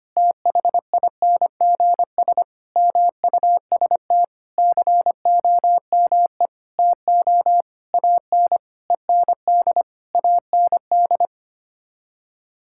Morse code, heard as a keyed tone, about 25 wpm, 700 Hz. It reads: THINGS MUST COME TO AN END AND
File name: Sets of 9 Words - 25wpm